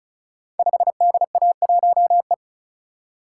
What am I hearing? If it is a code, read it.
Morse code, heard as a keyed tone, about 36 wpm, 700 Hz. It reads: 5DA1E
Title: Code Groups - 5 Letter:Numbers - 35wpm